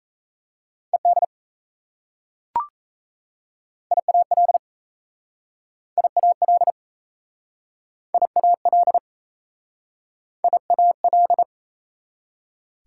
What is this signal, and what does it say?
Morse code, heard as a keyed tone, about 38 wpm, 690 Hz. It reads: ED IAL IAL IAL IAL